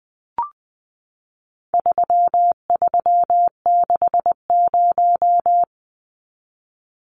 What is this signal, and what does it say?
Morse code, heard as a keyed tone, about 20 wpm, 700 Hz. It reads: 3360
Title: Numbers - 4-Digit 20wpm